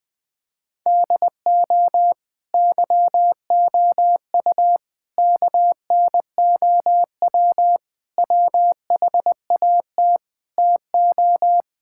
Morse code, heard as a keyed tone, about 21 wpm, 700 Hz. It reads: DO YOU KNOW WHAT TO
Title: Sentences with 2-Word Phrases - 20wpm